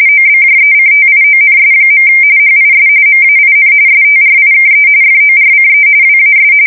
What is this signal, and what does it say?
RTTY, about 45.45 baud. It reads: OBJECTIONS. HIS MAIN STRIKE FORCE WAS
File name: fsk170 / usnavy4